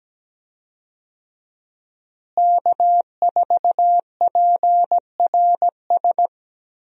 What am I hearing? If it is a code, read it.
Morse code, heard as a keyed tone, about 17 wpm, 700 Hz. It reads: K4PRS
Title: USA Call Signs - 5 Characters - Repeated 3x - 17wpm